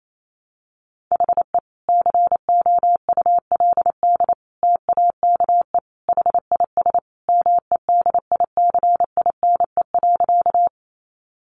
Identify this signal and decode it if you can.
Morse code, heard as a keyed tone, about 28 wpm, 700 Hz. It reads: HE COULD TAKE HIS MEDICINE.
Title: Sentences from Top 900 Words 28wpm